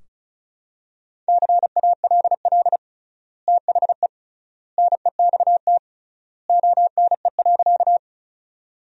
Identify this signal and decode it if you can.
Morse code, heard as a keyed tone, about 34 wpm, 700 Hz. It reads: CALL THE NEXT ONE.
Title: 3x Word-Spacing - Sentences from Top 300 Words - 35wpm